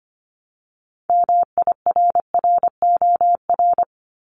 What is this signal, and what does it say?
Morse code, heard as a keyed tone, about 25 wpm, 700 Hz. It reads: MIRROR
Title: Multi-letter Phonemes Repeated 3x 25wpm